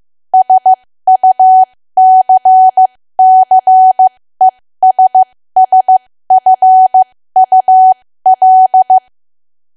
Morse code, heard as a keyed tone, about 15 wpm, spaced as for 13 wpm, 750 Hz. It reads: SUCCESSFUL
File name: cwa / W1AW-Level2 / 13WPM